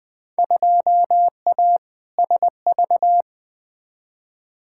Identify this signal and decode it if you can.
Morse code, heard as a keyed tone, about 21 wpm, 700 Hz. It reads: 2A SV
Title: Field Day Exchanges - 20wpm